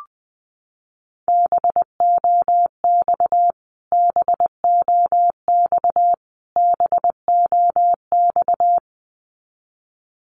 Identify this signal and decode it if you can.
Morse code, heard as a keyed tone, about 20 wpm, 700 Hz. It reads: BOX BOX BOX
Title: Top 500 Words Repeated 20wpm